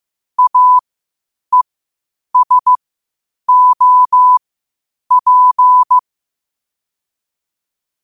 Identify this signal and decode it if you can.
Morse code, heard as a keyed tone, about 15 wpm, spaced as for 8 wpm, 1000 Hz. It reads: AESOP
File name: qotdfiles / 8wpm